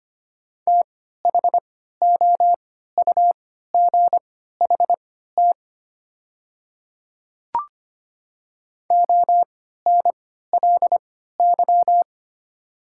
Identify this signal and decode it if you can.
Morse code, heard as a keyed tone, about 25 wpm, spaced as for 14 wpm, 700 Hz. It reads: THOUGHT ONLY
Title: 3x Character Spacing - Top 300 Words - 25wpm